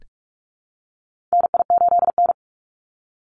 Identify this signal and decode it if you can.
Morse code, heard as a keyed tone, about 44 wpm, 700 Hz. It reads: NI8N
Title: Call Signs Repeated 2x - 45wpm